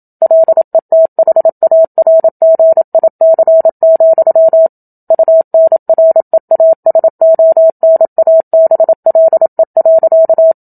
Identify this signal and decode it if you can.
Morse code, heard as a keyed tone, about 27 wpm, 650 Hz. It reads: LETHARGIC, UNREASONABLE.
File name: Book Two / 01-Chapter 1 - 27 WPM